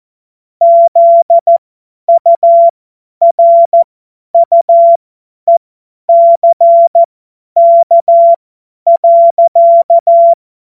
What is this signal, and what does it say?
Morse code, heard as a keyed tone, about 14 wpm, spaced as for 10 wpm, 680 Hz. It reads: ZURUECK.
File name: Morsekurs / Witze / Witz 17.Oktober 2023 50 F70